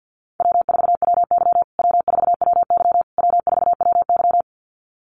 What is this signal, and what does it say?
Morse code, heard as a keyed tone, about 50 wpm, 700 Hz. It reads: W4WY W4WY W4WY
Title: USA Call Signs Repeated 3x 50wpm